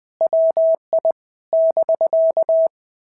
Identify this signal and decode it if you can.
Morse code, heard as a keyed tone, about 20 wpm, 650 Hz. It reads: WI <BK>
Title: POTA QSO Practice - 20wpm